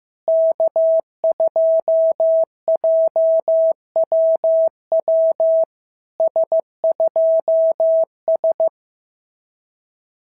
Morse code, heard as a keyed tone, about 15 wpm, 690 Hz. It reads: K2JWW S2S
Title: SOTA QSO Practice 15wpm